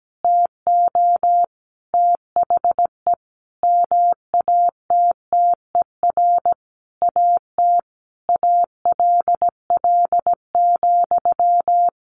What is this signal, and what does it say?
Morse code, heard as a keyed tone, about 17 wpm, 700 Hz. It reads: TO THE MATTER AT ALL,
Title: Sentences from Top 1200 Words 17wpm